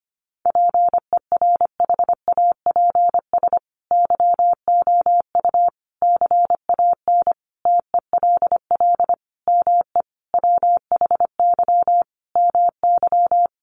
Morse code, heard as a keyed tone, about 25 wpm, 700 Hz. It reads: PERHAPS YOU CAN TELL ME WHY MY
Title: Sentences from Top 1200 Words 25wpm